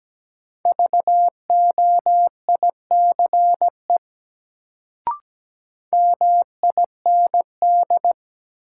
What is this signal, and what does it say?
Morse code, heard as a keyed tone, about 17 wpm, 700 Hz. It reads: VOICE MIND
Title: Sentences from Top 500 Words - New Word Drill - 17wpm